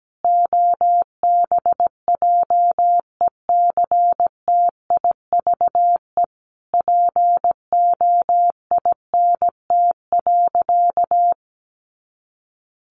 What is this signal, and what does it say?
Morse code, heard as a keyed tone, about 17 wpm, 700 Hz. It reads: OBJECTIVE POINT.
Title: Sentences from Top 1500 Words 17wpm